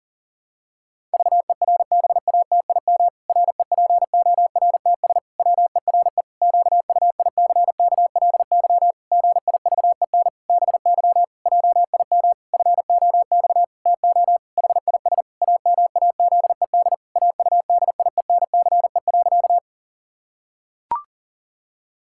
Morse code, heard as a keyed tone, about 40 wpm, 700 Hz. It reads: VERBATIM REPORTS WERE QUICKLY GIVEN BY JIM FOX TO HIS AMAZED AUDIENCE.
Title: Pangram Challenge v2 - 40wpm